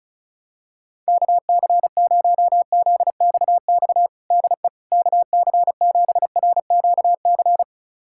Morse code, heard as a keyed tone, about 36 wpm, 700 Hz. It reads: KC0ZXX DE KC7RQC
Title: USA Callsign DE Callsign 35wpm